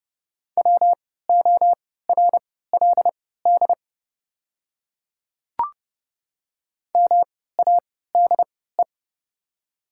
Morse code, heard as a keyed tone, about 29 wpm, spaced as for 17 wpm, 700 Hz. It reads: WORLD MADE
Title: 3x Character Spacing - Top 200 Words - 30wpm